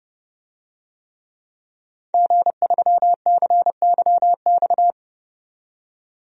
Morse code, heard as a keyed tone, about 30 wpm, 700 Hz. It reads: G3CYX
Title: DXCC-100 Call Signs - 5 Characters - 30wpm